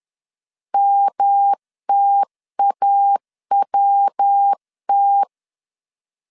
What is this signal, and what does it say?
Morse code, heard as a keyed tone, about 10 wpm, 800 Hz. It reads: MTAWT
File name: CW / DET / 1 3 1500C S C 10wpm 3335